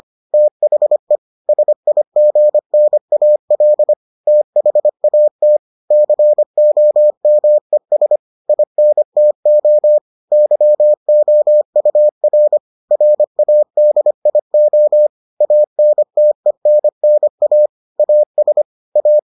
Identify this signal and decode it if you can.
Morse code, heard as a keyed tone, about 25 wpm, 600 Hz. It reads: THE SIGNAL THAT COMES INTO YOUR RADIO ANTENNA AS A